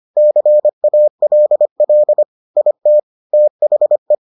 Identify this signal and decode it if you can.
Morse code, heard as a keyed tone, about 25 wpm, 600 Hz. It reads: CALL IT THE